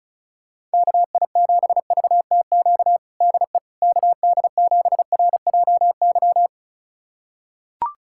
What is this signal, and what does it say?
Morse code, heard as a keyed tone, about 34 wpm, 700 Hz. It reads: KI7VTQ DE KD7RJY
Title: USA Callsign DE Callsign - 6 Characters - 35wpm